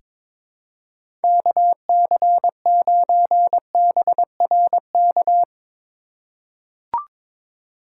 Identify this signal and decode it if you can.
Morse code, heard as a keyed tone, about 22 wpm, 700 Hz. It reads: KC9BRK
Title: USA Call Signs - 6 Characters - 2x Character Spacing - 22wpm